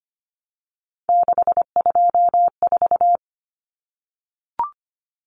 Morse code, heard as a keyed tone, about 25 wpm, 700 Hz. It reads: 624
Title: Numbers - 3-Digit 25wpm